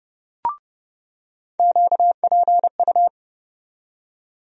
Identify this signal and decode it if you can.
Morse code, heard as a keyed tone, about 29 wpm, 700 Hz. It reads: QPU
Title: USA Call Sign Postfixes - 3 Letters 30wpm